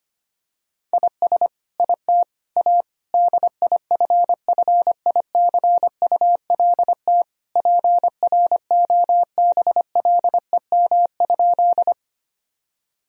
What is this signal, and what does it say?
Morse code, heard as a keyed tone, about 25 wpm, 700 Hz. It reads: IS IT A DIFFICULT PROBLEM?
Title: Sentences from Top 1000 Words - Review Part 2 - 25wpm